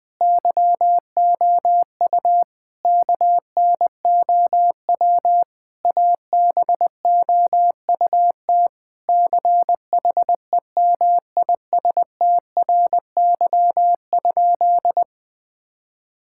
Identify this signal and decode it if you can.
Morse code, heard as a keyed tone, about 20 wpm, 700 Hz. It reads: YOU KNOW ABOUT CHEMISTRY?
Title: Sentences from Top 1000 Words - Review Part 6 - 20wpm